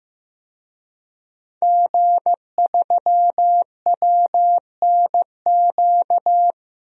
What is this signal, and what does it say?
Morse code, heard as a keyed tone, about 16 wpm, 700 Hz. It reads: G3WNQ
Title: DXCC-100 Call Signs - 5 Characters - Repeated 3x - 15wpm